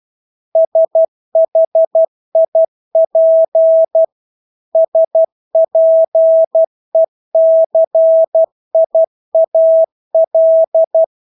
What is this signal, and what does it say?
Morse code, heard as a keyed tone, about 12 wpm, 650 Hz. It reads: SHIP SPECIAL
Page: Engelske ord 12 wpm